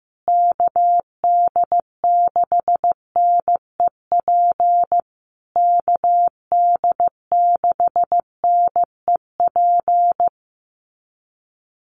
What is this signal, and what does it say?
Morse code, heard as a keyed tone, about 14 wpm, 700 Hz. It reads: KD6NEP KD6NEP
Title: USA Call Signs Repeated 3x 15wpm